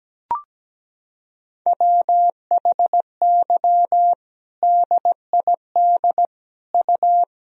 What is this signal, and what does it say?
Morse code, heard as a keyed tone, about 17 wpm, 700 Hz. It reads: WHY DID U
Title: Sentences from Top 500 Words and Abbreviations 17wpm